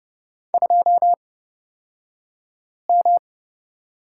Morse code, heard as a keyed tone, about 30 wpm, spaced as for 5 wpm, 700 Hz. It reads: 2M
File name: Lesson 38d - Letter q - Two Characters 30wpm